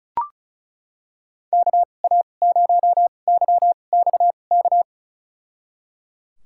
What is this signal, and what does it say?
Morse code, heard as a keyed tone, about 34 wpm, spaced as for 25 wpm, 700 Hz. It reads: KA0YXK
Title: USA Call Signs - 6 Characters - 2x Character Spacing - 35wpm